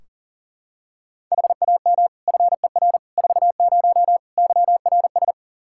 Morse code, heard as a keyed tone, about 38 wpm, 700 Hz. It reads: HAM FER 40 YRS
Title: Practice QSO - Exchange 3 - More Info - 40wpm